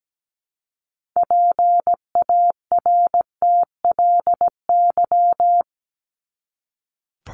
Morse code, heard as a keyed tone, about 17 wpm, 700 Hz. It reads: PARTLY
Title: LY Suffix - Top 100 Words with Base Words 17wpm